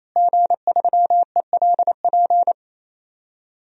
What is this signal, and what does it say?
Morse code, heard as a keyed tone, about 27 wpm, 700 Hz. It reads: G3ELP
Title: DXCC-100 Call Signs - 5 Characters - Repeated 3x - 28wpm